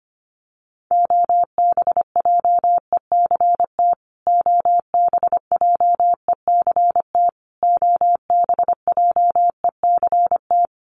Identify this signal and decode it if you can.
Morse code, heard as a keyed tone, about 25 wpm, 700 Hz. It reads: OBJECT OBJECT OBJECT
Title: Top 500 Words Repeated 25wpm